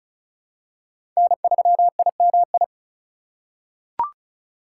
Morse code, heard as a keyed tone, about 36 wpm, 700 Hz. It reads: N3IMI
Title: USA Call Signs - 5 Characters - 4x Character Spacing - 35wpm